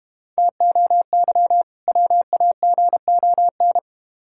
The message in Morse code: TOY WAGON

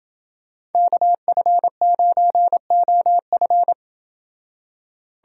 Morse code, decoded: KF9OF

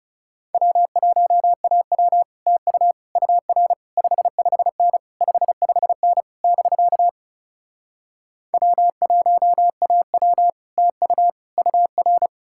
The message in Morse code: W1AW TU UR 55N 55N <BK> W1AW TU UR

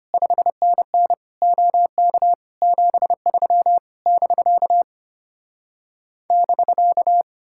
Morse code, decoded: 5NN OK 73 <BK> <BK>

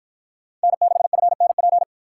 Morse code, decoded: N6FNP

